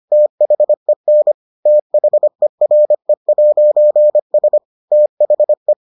Morse code, heard as a keyed tone, about 25 wpm, 600 Hz. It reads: THEN THERE'S THE